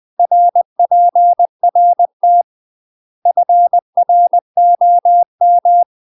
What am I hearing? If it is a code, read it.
Morse code, heard as a keyed tone, about 20 wpm, 700 Hz. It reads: RPRT FROM
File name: UK MASTER QSOs TRAINING / 20WPM / UK MASTER QSOs 0003